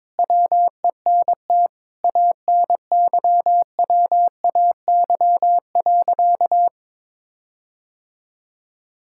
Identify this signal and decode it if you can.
Morse code, heard as a keyed tone, about 22 wpm, 700 Hz. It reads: WENT ANYWAY.
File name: practice / 22wpm / Sentences with 4-Word Phrases - 22wpm